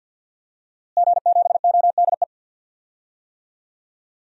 Morse code, heard as a keyed tone, about 52 wpm, 700 Hz. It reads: G7ODE
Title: DXCC-100 Call Signs - 5 Characters - Repeated 2x - 50wpm